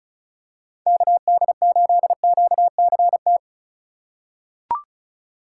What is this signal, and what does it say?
Morse code, heard as a keyed tone, about 36 wpm, 700 Hz. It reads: KD8QCT
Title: USA Call Signs - 6 Characters - 4x Character Spacing - 35wpm